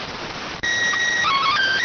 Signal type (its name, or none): SSTV (slow-scan television)